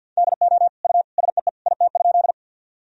Morse code, heard as a keyed tone, about 53 wpm, 700 Hz. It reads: DO U SEE IT?
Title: Sentences from Top 100 Words and Abbreviations 50wpm